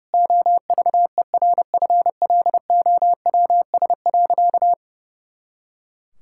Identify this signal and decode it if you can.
Morse code, heard as a keyed tone, about 30 wpm, 700 Hz. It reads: OVERFLOWS.
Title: Pangram Challenge v2 - 30wpm